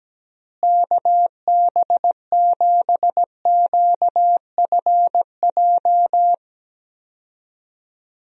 Morse code, heard as a keyed tone, about 17 wpm, 700 Hz. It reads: KB7QFJ
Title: USA Call Signs - 6 Characters - Repeated 2x - 17wpm